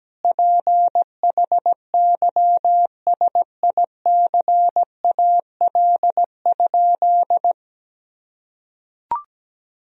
Morse code, heard as a keyed tone, about 17 wpm, 700 Hz. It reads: PHYSICAL?
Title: Sentences from Top 1000 Words - Review Part 4 - 17wpm